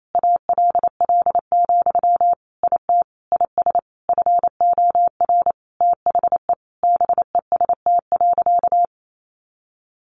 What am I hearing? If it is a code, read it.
Morse code, heard as a keyed tone, about 28 wpm, 700 Hz. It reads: ALL, IT IS FOR THE BEST.
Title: Sentences from Top 500 Words 28wpm